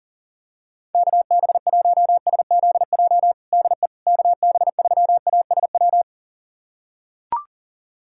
Morse code, heard as a keyed tone, about 40 wpm, 700 Hz. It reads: KB1SZJ DE KB3ASW
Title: USA Callsign DE Callsign - 6 Characters - 40wpm